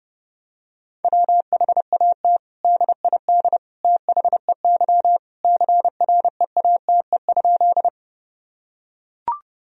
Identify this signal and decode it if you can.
Morse code, heard as a keyed tone, about 30 wpm, 700 Hz. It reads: WHAT DID THEY CREATE?
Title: Sentences from Top 1000 Words - Review Part 3 - 30wpm